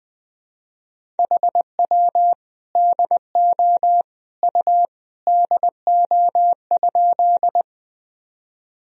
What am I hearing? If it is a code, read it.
Morse code, heard as a keyed tone, about 20 wpm, 700 Hz. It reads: HW DO U DO?
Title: Sentences from Top 100 Words and Abbreviations 20wpm